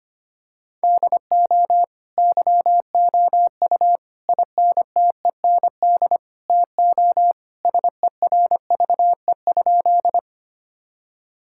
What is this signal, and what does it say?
Morse code, heard as a keyed tone, about 25 wpm, 700 Hz. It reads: DO YOU INTEND TO SERVE?
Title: Sentences from Top 1600 Words 25wpm